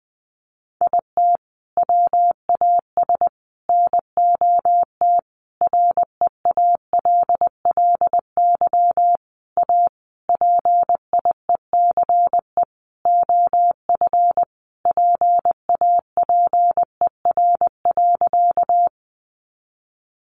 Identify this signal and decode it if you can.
Morse code, heard as a keyed tone, about 20 wpm, 700 Hz. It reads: IT WAS NOT REALLY A PIECE OF PAPER.
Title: Sentences from Top 600 Words 20wpm